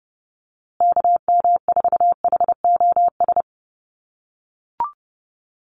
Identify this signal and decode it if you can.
Morse code, heard as a keyed tone, about 30 wpm, 700 Hz. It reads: KM4HOS
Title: USA Call Signs Repeated 3x 30wpm